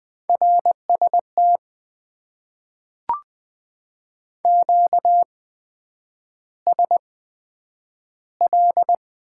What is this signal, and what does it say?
Morse code, heard as a keyed tone, about 20 wpm, spaced as for 9 wpm, 700 Hz. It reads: RST QSL
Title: Top 90 QSO Elements - 8x Character Spacing - 20wpm